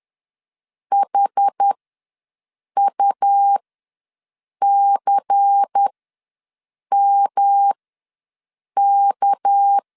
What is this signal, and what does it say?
Morse code, heard as a keyed tone, about 10 wpm, spaced as for 6 wpm, 800 Hz. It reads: HUCMK
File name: LEZIONI / 6 PKX0 / 6 PK ws 10wpm 3745